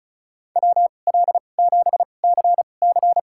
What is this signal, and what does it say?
Morse code, heard as a keyed tone, about 36 wpm, spaced as for 25 wpm, 700 Hz. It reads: WL7CC
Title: USA Call Signs - 5 Characters - 2x Character Spacing - 35wpm